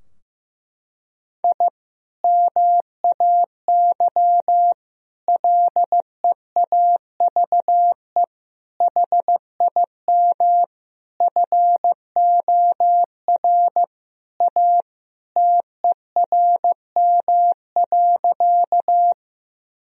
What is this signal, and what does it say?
Morse code, heard as a keyed tone, about 15 wpm, 700 Hz. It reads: I MAY LEAVE HIM FOR A TERM.
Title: Sentences from Top 1000 Words - Review Part 4 - 15wpm